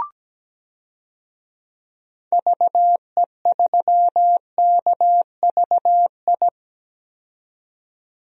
Morse code, heard as a keyed tone, about 17 wpm, 700 Hz. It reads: VE3KVI
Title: DXCC-100 Call Signs - 6 Characters - 17wpm